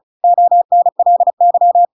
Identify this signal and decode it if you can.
Morse code, heard as a keyed tone, about 34 wpm, 700 Hz. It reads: ONLY
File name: Pinocchio / Pinocchio ch5 35 35 ik1hge